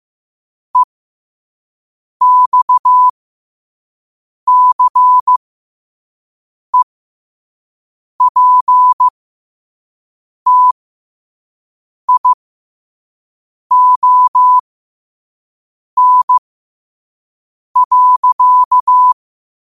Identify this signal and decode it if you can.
Morse code, heard as a keyed tone, about 15 wpm, spaced as for 5 wpm, 1000 Hz. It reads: EXCEPTION.
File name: qotdfiles / 5wpm